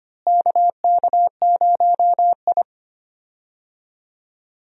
Morse code, heard as a keyed tone, about 25 wpm, 700 Hz. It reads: KK0I
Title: USA Call Signs - 4 Characters - Repeated 3x - 25wpm